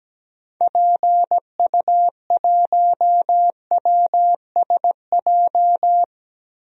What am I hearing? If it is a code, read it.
Morse code, heard as a keyed tone, about 17 wpm, 700 Hz. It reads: PU1WSJ
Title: DXCC-100 Call Signs - 6 Characters - Repeated 3x - 17wpm